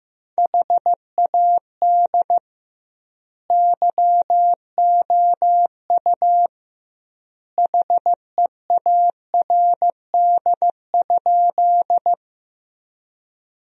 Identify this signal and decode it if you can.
Morse code, heard as a keyed tone, about 15 wpm, 700 Hz. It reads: HAD YOU HEARD?
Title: 2x Word-Spacing - Sentences from Top 400 Words - 15wpm